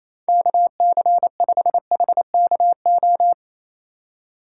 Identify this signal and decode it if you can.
Morse code, heard as a keyed tone, about 28 wpm, 700 Hz. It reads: KC5HKO
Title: USA Call Signs - 2x Character Spacing - 28wpm